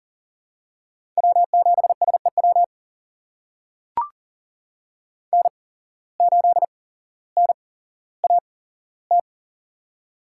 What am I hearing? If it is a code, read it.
Morse code, heard as a keyed tone, about 40 wpm, spaced as for 17 wpm, 700 Hz. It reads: W7SEW N8NAT
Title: USA Call Signs - 5 Characters - 8x Character Spacing - 40wpm